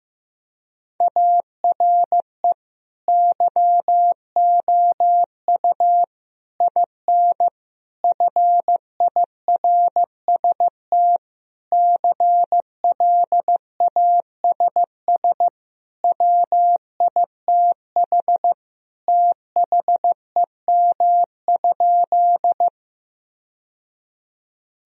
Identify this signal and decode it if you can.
Morse code, heard as a keyed tone, about 15 wpm, 700 Hz. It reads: ARE YOU IN FIRST CLASS WITH THEM?
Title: Sentences from Top 1000 Words - Review Part 2 - 15wpm